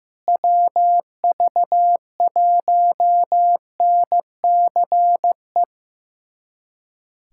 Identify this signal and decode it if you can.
Morse code, heard as a keyed tone, about 15 wpm, 700 Hz. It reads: WV1NCE
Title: USA Callsign DE Callsign - 6 Characters - 15wpm